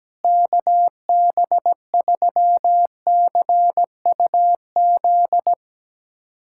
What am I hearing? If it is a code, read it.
Morse code, heard as a keyed tone, about 17 wpm, 700 Hz. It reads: KB3CUZ
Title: USA Callsign DE Callsign 17wpm